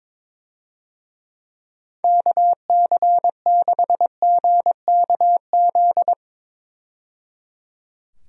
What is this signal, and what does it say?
Morse code, heard as a keyed tone, about 21 wpm, 700 Hz. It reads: KC6GKZ
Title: USA Call Signs - 6 Characters - 22wpm